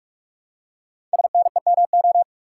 Morse code, heard as a keyed tone, about 44 wpm, 700 Hz. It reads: SNEMO
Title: Code Groups - 5 Letters - 45wpm